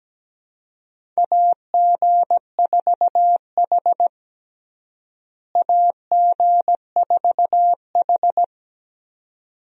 Morse code, heard as a keyed tone, about 17 wpm, 700 Hz. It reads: AG4H AG4H
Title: USA Call Signs - 4 Characters - Repeated 2x - 17wpm